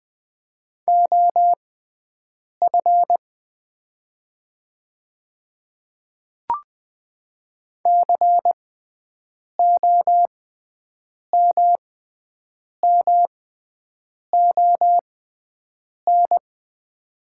Morse code, heard as a keyed tone, about 20 wpm, spaced as for 7 wpm, 690 Hz. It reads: OF COMMON
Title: 6x Character Spacing - Top 500 Words - 20wpm